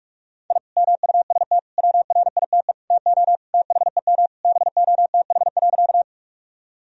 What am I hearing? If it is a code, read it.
Morse code, heard as a keyed tone, about 46 wpm, 700 Hz. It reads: I MUST WRITE TO THEM BOTH.
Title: Sentences from Top 1000 Words - Review Part 1 - 45wpm